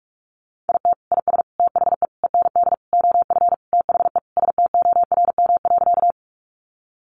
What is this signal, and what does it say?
Morse code, heard as a keyed tone, about 44 wpm, 700 Hz. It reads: IT IS THE END OF THE STORM.
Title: Sentences from Top 1500 Words 45wpm